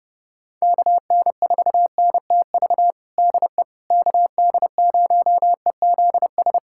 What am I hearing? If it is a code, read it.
Morse code, heard as a keyed tone, about 30 wpm, 700 Hz. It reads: KN4NTV DE KD0EZS
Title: USA Callsign DE Callsign 30wpm